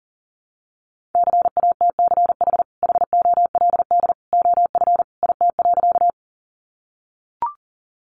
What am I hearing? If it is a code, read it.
Morse code, heard as a keyed tone, about 40 wpm, 700 Hz. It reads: CATCH HOLD OF IT.
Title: Sentences from Top 1500 Words 40wpm